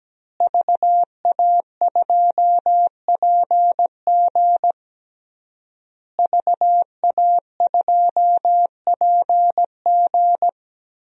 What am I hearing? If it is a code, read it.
Morse code, heard as a keyed tone, about 17 wpm, 700 Hz. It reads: VA2PG VA2PG
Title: DXCC-100 Call Signs - 5 Characters - Repeated 3x - 17wpm